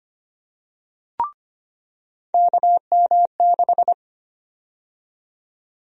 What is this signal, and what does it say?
Morse code, heard as a keyed tone, about 25 wpm, 700 Hz. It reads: KM6
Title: USA Call Sign Prefixes - 25wpm